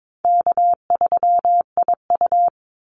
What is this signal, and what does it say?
Morse code, heard as a keyed tone, about 22 wpm, 700 Hz. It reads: K3IU
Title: Easy Call Signs Repeated 3x - 22wpm